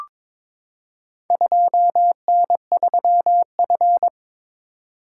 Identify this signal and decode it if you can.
Morse code, heard as a keyed tone, about 22 wpm, 700 Hz. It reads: 2N3F
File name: practice / 22wpm / Four Letter-Number Combinations - 22wpm